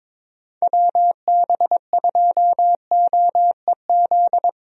Morse code, heard as a keyed tone, about 22 wpm, 700 Hz. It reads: WB2OEZ
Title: USA Call Signs - 6 Characters - Repeated 2x - 22wpm